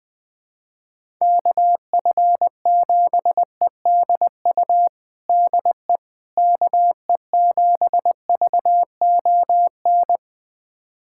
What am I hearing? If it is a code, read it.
Morse code, heard as a keyed tone, about 20 wpm, 700 Hz. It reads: KF7EDU DE KE7VON
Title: USA Callsign DE Callsign - 6 Characters - 20wpm